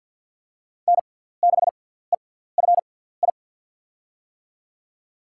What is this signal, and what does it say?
Morse code, heard as a keyed tone, about 48 wpm, spaced as for 17 wpm, 700 Hz. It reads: N6EFI
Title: USA Call Signs - 5 Characters - 6x Character Spacing - 50wpm